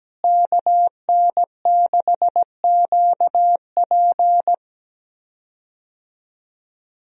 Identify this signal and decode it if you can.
Morse code, heard as a keyed tone, about 17 wpm, 700 Hz. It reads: KN6QP
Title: USA Call Signs - 5 Characters - Repeated 3x - 17wpm